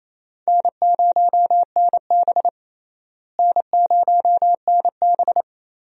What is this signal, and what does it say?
Morse code, heard as a keyed tone, about 28 wpm, 700 Hz. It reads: N0NB N0NB
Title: USA Call Signs - 4 Characters - Repeated 3x - 28wpm